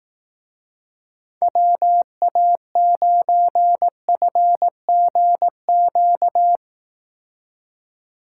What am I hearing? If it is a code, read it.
Morse code, heard as a keyed tone, about 18 wpm, 700 Hz. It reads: WA9FGQ
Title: Speed Racing - USA 2x3 Call Signs 15wpm